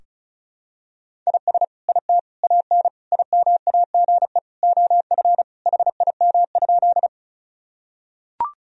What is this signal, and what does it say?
Morse code, heard as a keyed tone, about 36 wpm, 700 Hz. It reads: IS IT AN IMAGE OF HIM?
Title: Sentences from Top 1000 Words - Review Part 4 - 35wpm